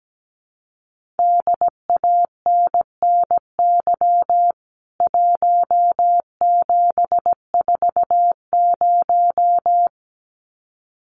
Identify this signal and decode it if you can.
Morse code, heard as a keyed tone, about 17 wpm, 700 Hz. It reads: DANNY 1740
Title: CWT Exchange 17wpm